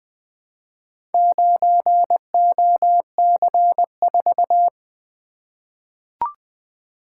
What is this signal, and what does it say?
Morse code, heard as a keyed tone, about 20 wpm, 700 Hz. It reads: 9OC4
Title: Four Letter-Number Combinations - 20wpm